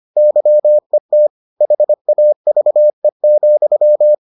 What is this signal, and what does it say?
Morse code, heard as a keyed tone, about 25 wpm, 600 Hz. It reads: YET HAVE,